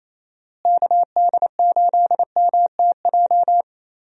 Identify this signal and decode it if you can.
Morse code, heard as a keyed tone, about 28 wpm, 700 Hz. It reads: KD8MTJ
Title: USA Call Signs - 6 Characters - Repeated 3x - 28wpm